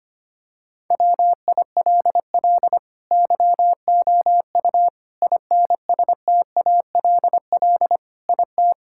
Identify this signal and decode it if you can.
Morse code, heard as a keyed tone, about 25 wpm, 700 Hz. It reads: WILL YOU INSTALL IT
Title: Sentences from Top 1600 Words 25wpm